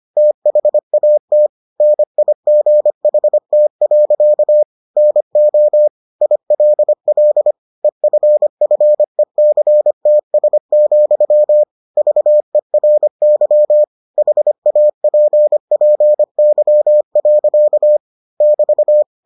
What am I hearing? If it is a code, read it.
Morse code, heard as a keyed tone, about 25 wpm, 600 Hz. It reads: THAT NIGHT. NO ILL EFFECTS, VERY HAPPY. <BT>